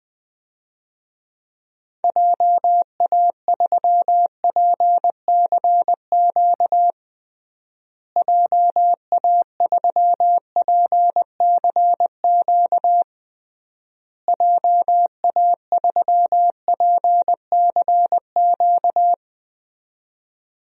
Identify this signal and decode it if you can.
Morse code, heard as a keyed tone, about 20 wpm, 700 Hz. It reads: JA3PCQ JA3PCQ JA3PCQ
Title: DXCC-100 Call Signs - 6 Characters - Repeated 3x - 20wpm